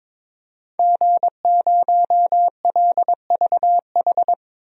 Morse code, heard as a keyed tone, about 22 wpm, 700 Hz. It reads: G0LVH
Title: DXCC-100 Call Signs - 5 Characters - Repeated 2x - 22wpm